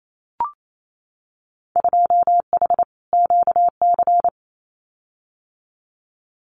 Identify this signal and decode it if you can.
Morse code, heard as a keyed tone, about 28 wpm, 700 Hz. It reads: 2H QC
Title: Winter Field Day Exchanges 28wpm